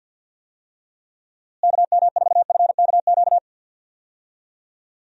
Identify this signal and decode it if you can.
Morse code, heard as a keyed tone, about 52 wpm, 700 Hz. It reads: KM4FKQ
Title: USA Call Signs - 6 Characters - Repeated 2x - 50wpm